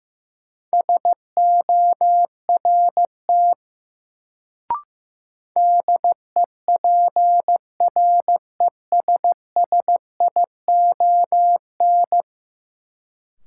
Morse code, heard as a keyed tone, about 15 wpm, 700 Hz. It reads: SORT DEPRESSION
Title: Sentences from Top 1300 Words - New Word Drill Alt - 15wpm